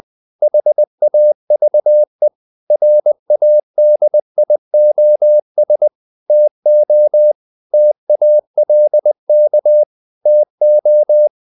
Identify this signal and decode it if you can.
Morse code, heard as a keyed tone, about 20 wpm, 600 Hz. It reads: HAVE RADIOS TO TALK TO
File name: the old man and the sea / the old man and the sea part 1 20wpm